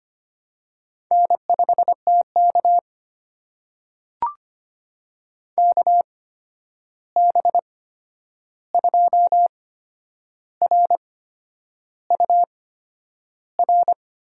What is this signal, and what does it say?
Morse code, heard as a keyed tone, about 25 wpm, spaced as for 9 wpm, 700 Hz. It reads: N5TK KB2RUR
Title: USA Call Signs - 8x Character Spacing - 25wpm